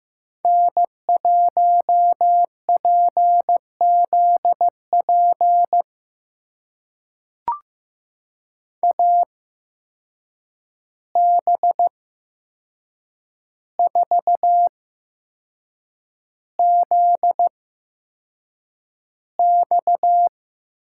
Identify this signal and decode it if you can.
Morse code, heard as a keyed tone, about 15 wpm, spaced as for 6 wpm, 700 Hz. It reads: N1PZP AB4ZX
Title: USA Call Signs - 5 Characters - 8x Character Spacing - 15wpm